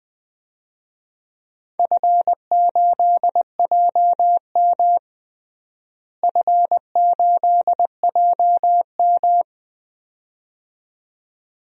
Morse code, heard as a keyed tone, about 21 wpm, 700 Hz. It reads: F8JM F8JM
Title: DXCC-100 Call Signs - 4 Characters - Repeated 2x - 20wpm